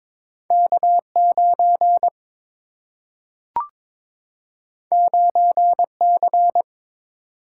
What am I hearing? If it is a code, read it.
Morse code, heard as a keyed tone, about 22 wpm, 700 Hz. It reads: K9 9C
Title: Single-Letter Single-Number 22wpm